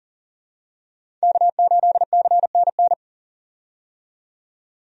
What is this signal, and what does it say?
Morse code, heard as a keyed tone, about 40 wpm, 700 Hz. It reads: K8CNN
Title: USA Call Signs - 5 Characters - 40wpm